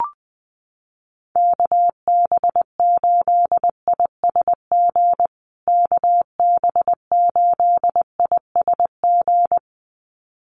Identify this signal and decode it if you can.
Morse code, heard as a keyed tone, about 20 wpm, 700 Hz. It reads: KB8ISG KB8ISG
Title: USA Call Signs Repeated 2x 20wpm